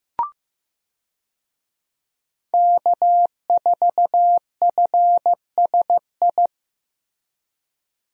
Morse code, heard as a keyed tone, about 15 wpm, 700 Hz. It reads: K4FSI
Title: USA Call Signs - 5 Characters - Repeated 2x - 15wpm